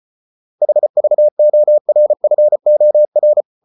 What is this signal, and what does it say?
Morse code, heard as a keyed tone, about 34 wpm, 600 Hz. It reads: HVORFOR
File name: Ferie QRQ 34wpm